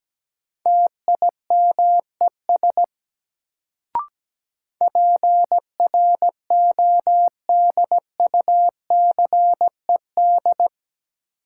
Morse code, heard as a keyed tone, about 17 wpm, 700 Hz. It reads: TIMES PRODUCED
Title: Sentences from Top 1000 Words - Review Part 6 - New Word Drill - 17wpm